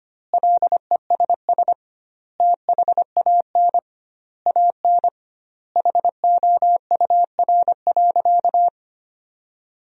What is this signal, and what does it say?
Morse code, heard as a keyed tone, about 25 wpm, 700 Hz. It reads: LESS THAN AN HOUR.
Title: 2x Word-Spacing - Sentences from Top 500 Words - 25wpm